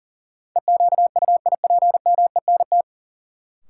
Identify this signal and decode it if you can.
Morse code, heard as a keyed tone, about 38 wpm, 700 Hz. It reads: EQUIPMENT
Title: Sentences from Top 700 Words - New Word Drill - 40wpm